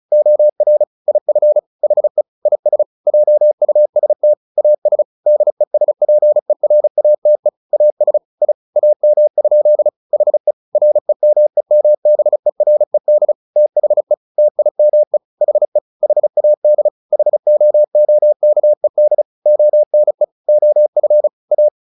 Morse code, heard as a keyed tone, about 34 wpm, 600 Hz. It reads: OR IF HE IS JUST AS DESPERATE AS I AM? HE REMEMBERED THE TIME HE HAD HOOKED ONE OF A
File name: the old man and the sea / the old man and the sea 1 35wpm